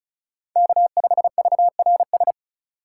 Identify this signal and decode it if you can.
Morse code, heard as a keyed tone, about 36 wpm, 700 Hz. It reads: K5VRS
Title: USA Call Signs - 5 Characters - 6x Character Spacing - 35wpm